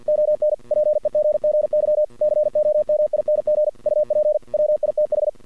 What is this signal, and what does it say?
Morse code, heard as a keyed tone, about 50 wpm, 620 Hz. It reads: ON PEGGY COMING UP WITH
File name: archived / CRSnet